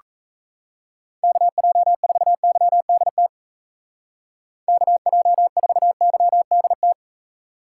Morse code, heard as a keyed tone, about 34 wpm, 700 Hz. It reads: KJ4YDT KJ4YDT
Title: Speed Racing - USA 2x3 Call Signs 28wpm